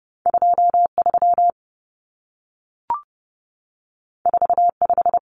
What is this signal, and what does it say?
Morse code, heard as a keyed tone, about 30 wpm, 700 Hz. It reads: 23 45
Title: Numbers - 2-Digit 30wpm